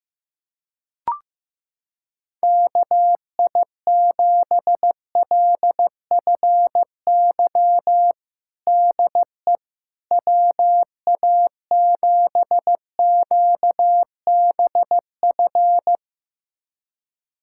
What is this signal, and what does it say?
Morse code, heard as a keyed tone, about 15 wpm, 700 Hz. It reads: KI7LFY DE WA7QBF
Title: USA Callsign DE Callsign 15wpm